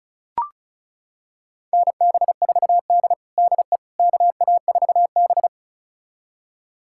Morse code, heard as a keyed tone, about 36 wpm, 700 Hz. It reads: NB4D DE KA4B
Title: USA Callsign DE Callsign - 4 Characters - 35wpm